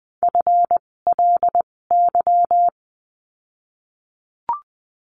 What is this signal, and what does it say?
Morse code, heard as a keyed tone, about 20 wpm, spaced as for 16 wpm, 700 Hz. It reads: FLY
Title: 256 Lesson 42a - Words 12-20wpm